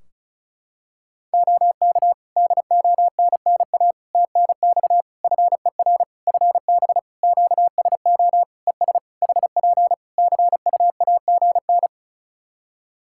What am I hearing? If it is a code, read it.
Morse code, heard as a keyed tone, about 34 wpm, 700 Hz. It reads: OK DONNA TNX FER FB QSO ES HP CUAGN
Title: Practice QSO - Exchange 4 - Ending - 35wpm